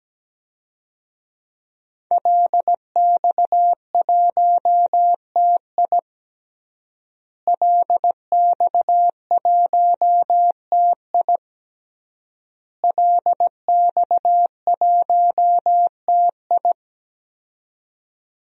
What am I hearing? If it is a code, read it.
Morse code, heard as a keyed tone, about 17 wpm, 700 Hz. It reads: LX1TI LX1TI LX1TI
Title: DXCC-100 Call Signs - 5 Characters - Repeated 3x - 17wpm